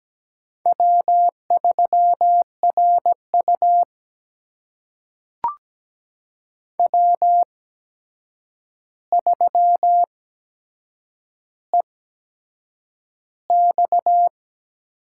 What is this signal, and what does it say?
Morse code, heard as a keyed tone, about 17 wpm, spaced as for 7 wpm, 700 Hz. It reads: W3RU W3EX
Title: USA Call Signs - 8x Character Spacing - 17wpm